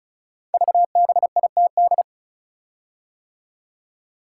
Morse code, heard as a keyed tone, about 36 wpm, 700 Hz. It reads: VBITD
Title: Code Groups - 5 Letters - 35wpm